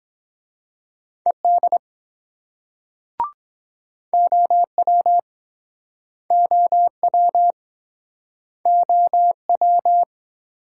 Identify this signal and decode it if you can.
Morse code, heard as a keyed tone, about 24 wpm, 700 Hz. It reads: ED OW OW OW